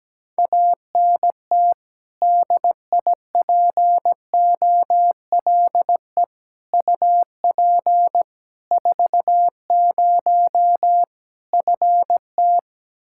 Morse code, heard as a keyed tone, about 17 wpm, 700 Hz. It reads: ANT DIPOLE UP 40 FT
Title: Practice QSO - Exchange 2 - Info - 17wpm